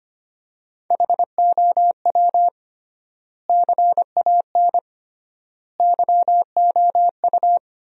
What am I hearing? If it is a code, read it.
Morse code, heard as a keyed tone, about 25 wpm, 700 Hz. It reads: HOW CAN YOU
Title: 3x Word-Spacing - Sentences from Top 200 Words - 25wpm